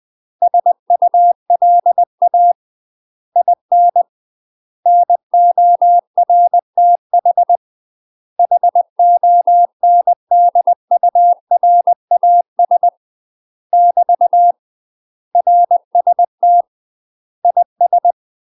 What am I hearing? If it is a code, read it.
Morse code, heard as a keyed tone, about 20 wpm, 700 Hz. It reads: SULA IN NORTH HONDURAS <BT> RST IS
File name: UK MASTER QSOs TRAINING / 20WPM / UK MASTER QSOs 0011